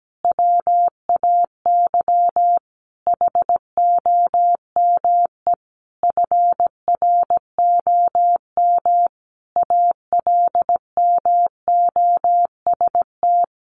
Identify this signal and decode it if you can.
Morse code, heard as a keyed tone, about 17 wpm, 700 Hz. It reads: WAY HOME FROM ALMOST
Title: Sentences from Top 1400 Words 17wpm